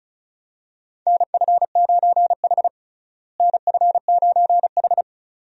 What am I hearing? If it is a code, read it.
Morse code, heard as a keyed tone, about 34 wpm, 700 Hz. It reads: NF9H NF9H
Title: USA Call Signs - 4 Characters - Repeated 2x - 35wpm